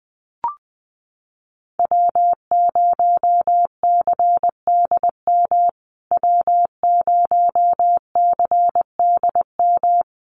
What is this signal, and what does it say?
Morse code, heard as a keyed tone, about 20 wpm, 700 Hz. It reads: W0CDM W0CDM
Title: USA Call Signs Repeated 3x 20wpm